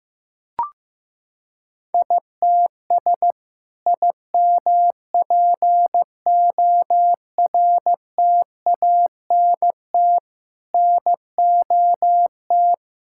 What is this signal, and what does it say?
Morse code, heard as a keyed tone, about 15 wpm, 700 Hz. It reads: ITS IMPORTANT NOT
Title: Sentences from Top 1000 Words - Review Part 5 - 15wpm